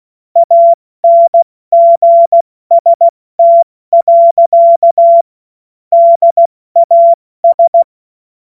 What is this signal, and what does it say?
Morse code, heard as a keyed tone, about 16 wpm, 680 Hz. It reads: ANGST. DAS
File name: Morsekurs / Witze / Witz 07. Februar 2023 70 F80